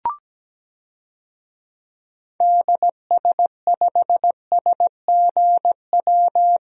Morse code, heard as a keyed tone, about 17 wpm, 700 Hz. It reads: DS5SGW
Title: DXCC-100 Call Signs - 6 Characters - Repeated 3x - 17wpm